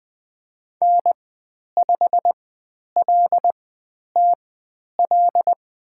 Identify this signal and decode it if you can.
Morse code, heard as a keyed tone, about 20 wpm, spaced as for 10 wpm, 700 Hz. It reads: N5LTL
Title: 102 Lesson 17g - Letter u - Call Signs 20wpm